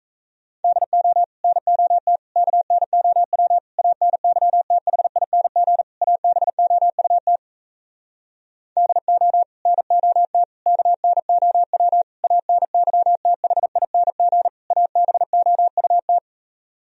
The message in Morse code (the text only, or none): DO NOT KNOW ANYTHING ABOUT DO NOT KNOW ANYTHING ABOUT